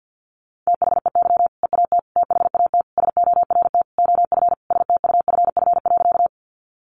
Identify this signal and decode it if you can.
Morse code, heard as a keyed tone, about 50 wpm, 700 Hz. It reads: THEY EAT THAT SORT OF STUFF.
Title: Sentences from Top 1400 Words 50wpm